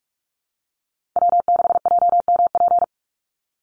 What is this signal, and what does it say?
Morse code, heard as a keyed tone, about 46 wpm, 700 Hz. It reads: W6JMP
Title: NA SOTA Chaser Callsigns 45wpm